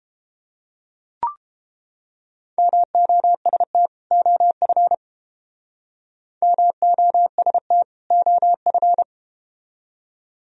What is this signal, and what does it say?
Morse code, heard as a keyed tone, about 32 wpm, 700 Hz. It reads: MOST OF MOST OF